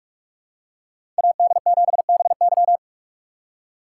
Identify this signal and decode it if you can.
Morse code, heard as a keyed tone, about 45 wpm, 700 Hz. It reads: AD7BY
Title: USA Call Signs - 5 Characters - 6x Character Spacing - 45wpm